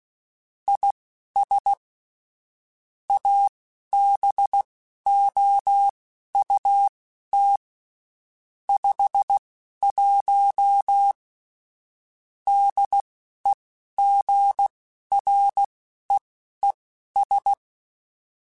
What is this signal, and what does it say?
Morse code, heard as a keyed tone, about 16 wpm, spaced as for 11 wpm, 800 Hz. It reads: IS ABOUT 51 DEGREES
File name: Spaced Characters / Spaced Characters-mp3s / Weather4 21 sp @15